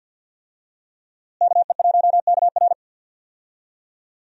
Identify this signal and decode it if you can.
Morse code, heard as a keyed tone, about 46 wpm, 700 Hz. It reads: KE1KR
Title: USA Call Signs - 5 Characters - Repeated 2x - 50wpm